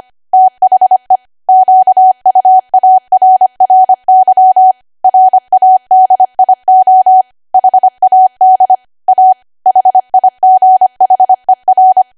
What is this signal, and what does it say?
Morse code, heard as a keyed tone, about 25 wpm, 750 Hz. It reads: THE QUARRY RADIO HAD A HIGHER
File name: Archive / 25 WPM / 25WPM